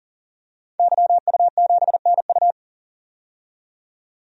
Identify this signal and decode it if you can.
Morse code, heard as a keyed tone, about 40 wpm, 700 Hz. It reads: YU7NU
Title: DXCC-100 Call Signs - 5 Characters - 40wpm